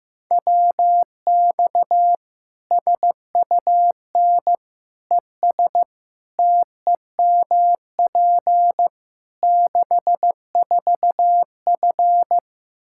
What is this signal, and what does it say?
Morse code, heard as a keyed tone, about 15 wpm, 700 Hz. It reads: WX SUN ES TEMP 64F
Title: Practice QSO - Exchange 2 - Info - 15wpm